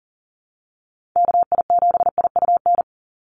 Morse code, heard as a keyed tone, about 42 wpm, 700 Hz. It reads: KI7IUN
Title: USA Call Signs 40wpm